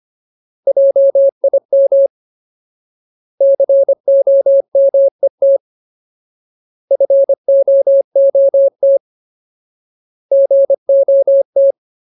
Morse code, heard as a keyed tone, about 25 wpm, 550 Hz. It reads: JIM COMET FOOT GOT
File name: Ultimate Word List / 25WPM Ultimate